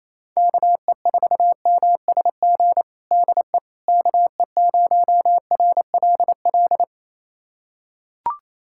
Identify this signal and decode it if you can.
Morse code, heard as a keyed tone, about 28 wpm, 700 Hz. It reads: KE4MSG DE KE0RLL
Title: USA Callsign DE Callsign - 6 Characters - 28wpm